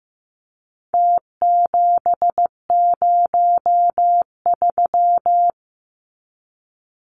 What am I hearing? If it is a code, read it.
Morse code, heard as a keyed tone, about 15 wpm, 700 Hz. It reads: T703
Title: Numbers - Contest Style - 4-Digit 15wpm